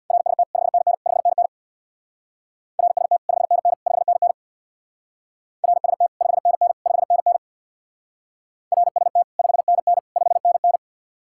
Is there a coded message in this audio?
Yes: Morse code from 0.1 to 10.8 s, heard as a keyed tone, about 66 wpm, 700 Hz. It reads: RST 5NN 5NN RST 5NN 5NN RST 5NN 5NN RST 5NN 5NN